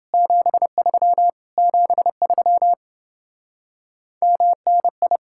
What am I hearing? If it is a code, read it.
Morse code, heard as a keyed tone, about 28 wpm, 700 Hz. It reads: 73 73 MNI